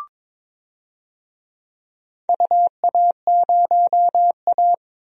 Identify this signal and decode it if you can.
Morse code, heard as a keyed tone, about 22 wpm, 700 Hz. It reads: UA0A
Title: DXCC-100 Call Signs - 4 Characters - Repeated 3x - 22wpm